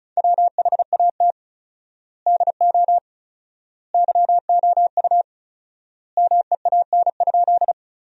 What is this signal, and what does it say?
Morse code, heard as a keyed tone, about 34 wpm, 700 Hz. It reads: WHAT DO YOU MEAN?
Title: 4x Word-Spacing - Sentences from Top 200 Words - 35wpm